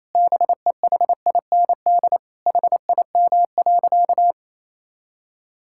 Morse code, heard as a keyed tone, about 28 wpm, 700 Hz. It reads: BEHIND HIM.
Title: Sentences from Top 1600 Words 28wpm